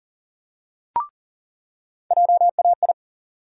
Morse code, heard as a keyed tone, about 40 wpm, 700 Hz. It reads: JAI
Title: USA Call Sign Postfixes - 3 Letters 40wpm